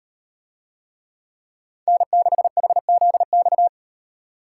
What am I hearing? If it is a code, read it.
Morse code, heard as a keyed tone, about 37 wpm, 700 Hz. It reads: N6HZX